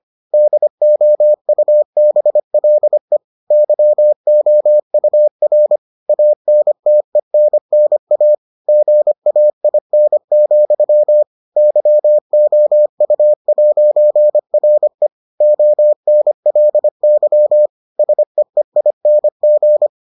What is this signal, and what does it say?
Morse code, heard as a keyed tone, about 25 wpm, 600 Hz. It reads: DOUBLE YOUR ANTENNA GAIN, YOU'RE ONLY SEEING